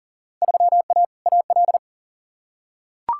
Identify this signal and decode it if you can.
Morse code, heard as a keyed tone, about 40 wpm, 700 Hz. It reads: 3A AL
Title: Field Day Exchanges - 40wpm